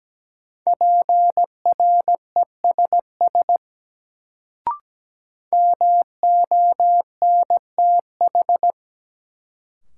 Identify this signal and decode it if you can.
Morse code, heard as a keyed tone, about 17 wpm, 700 Hz. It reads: PRESS MONTH
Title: Sentences from Top 800 Words - New Word Drill - 17wpm